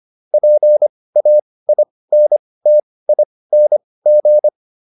Morse code, heard as a keyed tone, about 25 wpm, spaced as for 18 wpm, 600 Hz. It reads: PAINTING